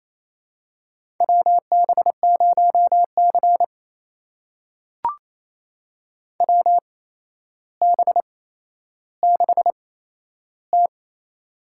Morse code, heard as a keyed tone, about 28 wpm, spaced as for 12 wpm, 700 Hz. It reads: WB0C WB6T
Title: USA Call Signs - 4 Characters - 8x Character Spacing - 28wpm